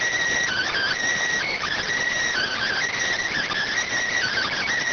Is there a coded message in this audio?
Yes: Emergency Alert System (SAME) tones from 0.5 to 4.6 s, heard as harsh buzzing bursts.